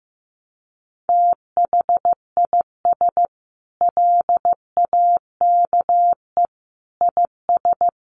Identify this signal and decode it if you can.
Morse code, heard as a keyed tone, about 15 wpm, 700 Hz. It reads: THIS LAKE IS
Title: Sentences from Top 1100 Words 15wpm